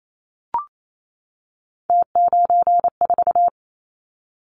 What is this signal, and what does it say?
Morse code, heard as a keyed tone, about 28 wpm, 700 Hz. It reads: T94
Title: Callsign Prefixes for Top 50 DXCC - 28wpm